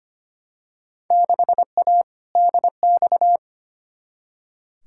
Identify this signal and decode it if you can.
Morse code, heard as a keyed tone, about 25 wpm, 700 Hz. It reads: 6A DX
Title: Field Day Exchanges Encore - 25wpm